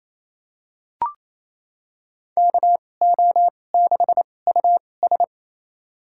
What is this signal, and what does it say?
Morse code, heard as a keyed tone, about 28 wpm, spaced as for 20 wpm, 700 Hz. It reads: KO6US
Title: USA Call Signs - 2x Character Spacing - 28wpm